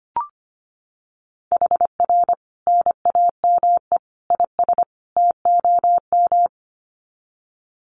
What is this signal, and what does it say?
Morse code, heard as a keyed tone, about 25 wpm, 700 Hz. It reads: HR NAME IS TOM
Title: Common QSO Phrases 25wpm